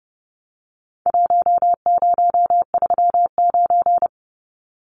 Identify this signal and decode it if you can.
Morse code, heard as a keyed tone, about 29 wpm, 700 Hz. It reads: 1039
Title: Numbers - Contest Style - 4-Digit 30wpm